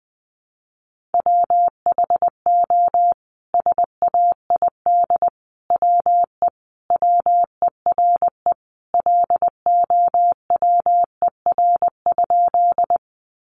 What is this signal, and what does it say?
Morse code, heard as a keyed tone, about 20 wpm, 700 Hz. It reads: WHO SAID WE WERE LOWER?
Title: Sentences from Top 700 Words 20wpm